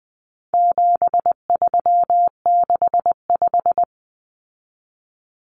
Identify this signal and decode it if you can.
Morse code, heard as a keyed tone, about 20 wpm, 700 Hz. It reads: 7365
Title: Common Cut Numbers - 4-Digit 20wpm